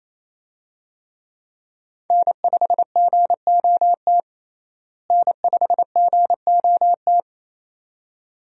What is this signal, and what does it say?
Morse code, heard as a keyed tone, about 27 wpm, 700 Hz. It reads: N5GOT N5GOT
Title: USA Call Signs - 5 Characters - Repeated 2x - 28wpm